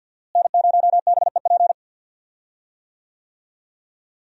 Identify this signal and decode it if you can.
Morse code, heard as a keyed tone, about 47 wpm, 700 Hz. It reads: N0BEP
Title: USA Call Signs - 5 Characters - Repeated 3x - 50wpm